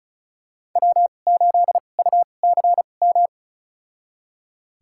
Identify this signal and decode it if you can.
Morse code, heard as a keyed tone, about 34 wpm, spaced as for 25 wpm, 700 Hz. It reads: W8UCM
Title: USA Call Signs - 5 Characters - 2x Character Spacing - 35wpm